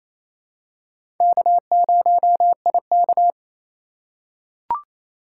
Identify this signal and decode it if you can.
Morse code, heard as a keyed tone, about 28 wpm, 700 Hz. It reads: K0IK
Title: USA Call Signs - 4 Characters - 4x Character Spacing - 28wpm